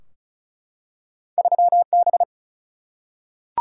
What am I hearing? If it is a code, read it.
Morse code, heard as a keyed tone, about 34 wpm, 700 Hz. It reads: 3B
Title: Single-Letter Single-Number 35wpm